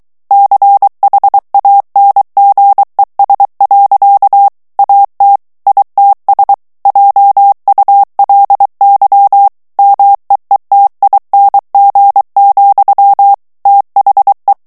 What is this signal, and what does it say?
Morse code, heard as a keyed tone, about 23 wpm, 800 Hz. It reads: CHANGES. AT ITS JULY MEETING, THE
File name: ANGLAIS texte clair 3 vitesse 22 mots minute